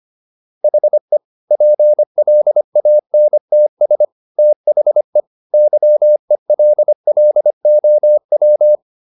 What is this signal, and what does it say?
Morse code, heard as a keyed tone, about 25 wpm, 600 Hz. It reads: HE PLANTS THE YELLOW
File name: SS111 25